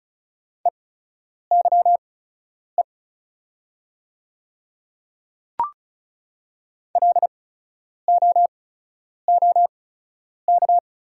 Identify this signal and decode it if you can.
Morse code, heard as a keyed tone, about 36 wpm, spaced as for 10 wpm, 700 Hz. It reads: EYE LOOK
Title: 8x Character Spacing - Top 300 Words - 35wpm